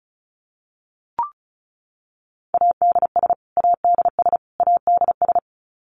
Morse code, heard as a keyed tone, about 34 wpm, 700 Hz. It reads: ADS ADS ADS
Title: Abbreviations Repeated 3x - 35wpm